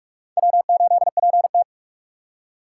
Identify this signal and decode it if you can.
Morse code, heard as a keyed tone, about 46 wpm, 700 Hz. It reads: W8PT
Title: USA Call Signs - 2x Character Spacing - 45wpm